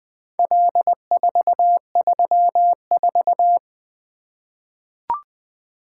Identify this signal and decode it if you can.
Morse code, heard as a keyed tone, about 20 wpm, 700 Hz. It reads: L434
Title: Four Letter-Number Combinations - 20wpm